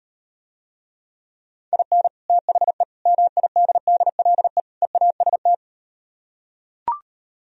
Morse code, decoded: IN THE MIDDLE EAST

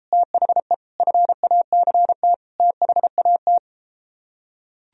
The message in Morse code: THE FACT THAT

